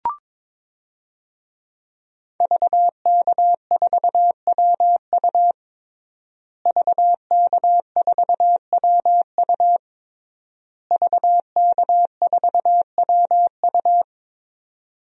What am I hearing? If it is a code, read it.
Morse code, heard as a keyed tone, about 22 wpm, 700 Hz. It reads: VK4WU VK4WU VK4WU
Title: DXCC-100 Call Signs - 5 Characters - Repeated 3x - 22wpm